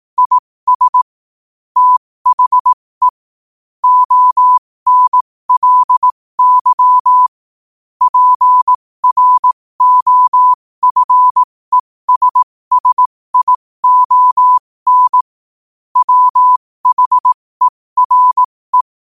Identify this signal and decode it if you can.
Morse code, heard as a keyed tone, about 18 wpm, spaced as for 15 wpm, 1000 Hz. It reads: IS THE ONLY PROFESSION WHERE
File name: qotdfiles / 15wpm